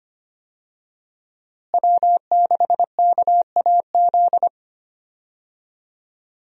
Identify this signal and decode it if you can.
Morse code, heard as a keyed tone, about 25 wpm, 700 Hz. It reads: W6KAZ
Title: USA Call Signs - 5 Characters - Repeated 3x - 25wpm